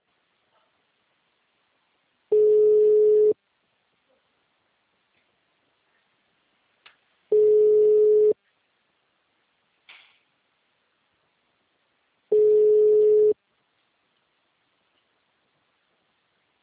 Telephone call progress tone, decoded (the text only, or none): ringback tone